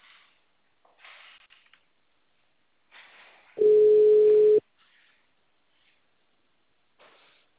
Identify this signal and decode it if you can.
Ringback tone (Europe, 425 Hz).
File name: out- -2016- - - .24080